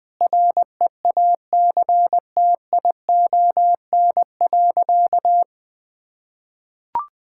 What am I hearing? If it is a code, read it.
Morse code, heard as a keyed tone, about 19 wpm, 700 Hz. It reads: REACTION.
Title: Sentences with 8-Word Phrases - Spoken Beginning - 20wpm